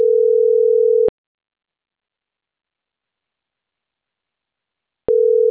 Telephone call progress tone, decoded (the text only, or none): ringback tone